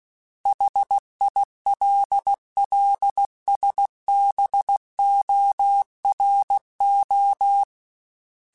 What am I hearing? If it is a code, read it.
Morse code, heard as a keyed tone, about 16 wpm, 800 Hz. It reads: HILLSBORO
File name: fromSMs / FromSMs mp3s / Cities@15